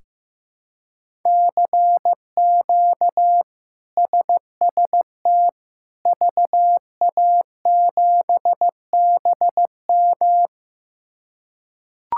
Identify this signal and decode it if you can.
Morse code, heard as a keyed tone, about 16 wpm, 700 Hz. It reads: CQ SST VA7BM
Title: K1USN SST QSO Practice - 15wpm